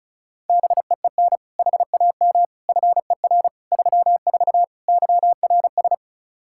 Morse code, heard as a keyed tone, about 36 wpm, 700 Hz. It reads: BEEN HAM FER 34 YRS
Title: Practice QSO - Exchange 3 - More Info - 35wpm